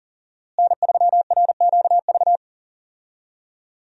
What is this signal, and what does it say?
Morse code, heard as a keyed tone, about 40 wpm, 700 Hz. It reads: N3RQV
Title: USA Call Signs - 5 Characters - Repeated 3x - 40wpm